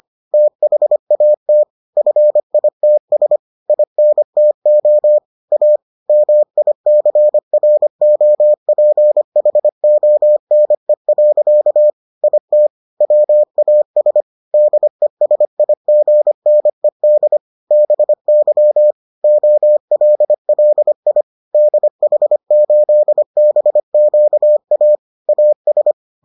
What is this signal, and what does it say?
Morse code, heard as a keyed tone, about 25 wpm, 600 Hz. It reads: THAT FITS INTO A MICROPHONE. IT WAS DESIGNED BY OLLI DH8BQA AS